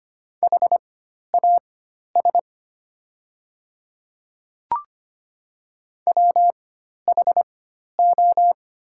Morse code, heard as a keyed tone, about 25 wpm, spaced as for 12 wpm, 700 Hz. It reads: HAS WHO
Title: 4x Character Spacing - Top 200 Words - 25wpm